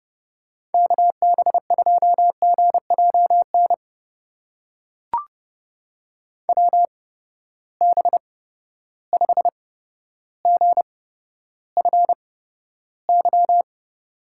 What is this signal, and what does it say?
Morse code, heard as a keyed tone, about 30 wpm, spaced as for 13 wpm, 700 Hz. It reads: KB2GJN WB5GFY
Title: USA Call Signs - 6 Characters - 8x Character Spacing - 30wpm